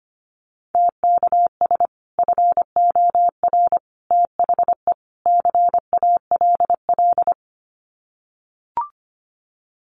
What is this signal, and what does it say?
Morse code, heard as a keyed tone, about 25 wpm, 700 Hz. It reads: TKS FOR THE CALL
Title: Common QSO Phrases 25wpm